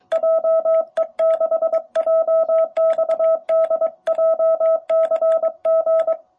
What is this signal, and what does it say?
Morse code, heard as a keyed tone, about 21 wpm, 660 Hz. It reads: JE6JXD JCG
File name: jh8cbh / 2021cw